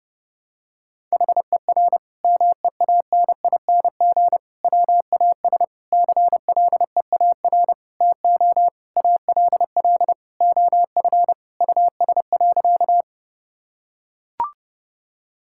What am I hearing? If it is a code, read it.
Morse code, heard as a keyed tone, about 32 wpm, 700 Hz. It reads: HER MEANING WAS CLEAR TO ALL OF US.
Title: Sentences from Top 1000 Words - Review Part 6 - 30wpm